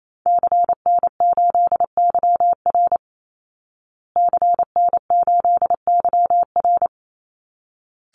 Morse code, heard as a keyed tone, about 28 wpm, 700 Hz. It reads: CN8YR CN8YR
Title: Call Signs Repeated 3x - 28wpm